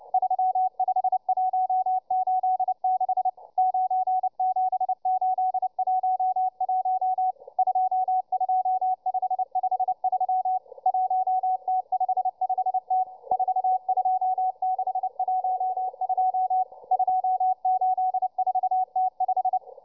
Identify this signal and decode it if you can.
Morse code, heard as a keyed tone, about 29 wpm, 750 Hz, with radio noise. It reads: 35186 97811 22553 1T55T 42612 284T5